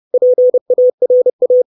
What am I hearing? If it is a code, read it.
Morse code, heard as a keyed tone, about 30 wpm, 500 Hz. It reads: PARA
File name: 632 words / 632 Words 30WPM